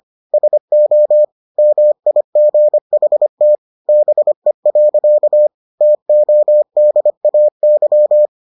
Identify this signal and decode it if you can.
Morse code, heard as a keyed tone, about 25 wpm, 600 Hz. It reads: SO MIGHT BE. TODAY